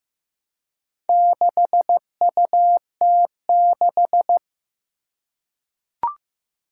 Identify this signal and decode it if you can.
Morse code, heard as a keyed tone, about 15 wpm, 700 Hz. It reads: 6UT6
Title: Four Challenging Letter-Number Combinations 15wpm